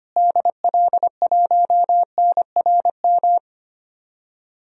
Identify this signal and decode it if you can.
Morse code, heard as a keyed tone, about 25 wpm, 700 Hz. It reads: DL1NRM
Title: DXCC-100 Call Signs - 6 Characters - Repeated 2x - 25wpm